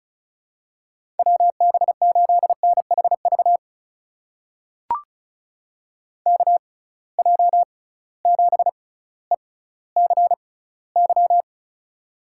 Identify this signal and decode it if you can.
Morse code, heard as a keyed tone, about 36 wpm, spaced as for 18 wpm, 690 Hz. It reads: WB8NHV KJ7ECY
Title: USA Call Signs - 6 Characters - 6x Character Spacing - 35wpm